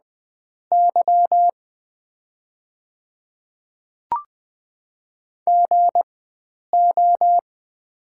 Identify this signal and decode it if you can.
Morse code, heard as a keyed tone, about 20 wpm, spaced as for 9 wpm, 700 Hz. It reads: Y GO
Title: 4x Character Spacing - Top 300 Words - 20wpm